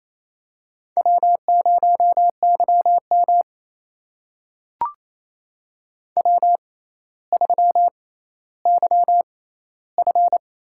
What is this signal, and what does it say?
Morse code, heard as a keyed tone, about 27 wpm, spaced as for 14 wpm, 700 Hz. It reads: W0YM W3YF
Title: USA Call Signs - 4 Characters - 6x Character Spacing - 28wpm